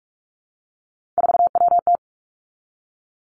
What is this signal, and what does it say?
Morse code, heard as a keyed tone, about 46 wpm, 700 Hz. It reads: 4WT
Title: Three Challenging Letter-Number Combinations 45wpm